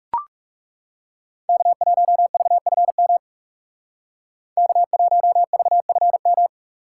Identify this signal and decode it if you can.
Morse code, heard as a keyed tone, about 39 wpm, 700 Hz. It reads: K1VFM K1VFM